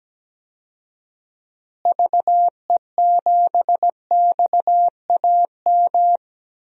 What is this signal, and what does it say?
Morse code, heard as a keyed tone, about 17 wpm, 700 Hz. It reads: VE7XAM
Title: DXCC-100 Call Signs - 6 Characters - Repeated 3x - 17wpm